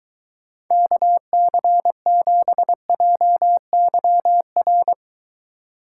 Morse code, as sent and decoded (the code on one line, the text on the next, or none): -.- -.-. --... .--- -.-- .-.
KC7JYR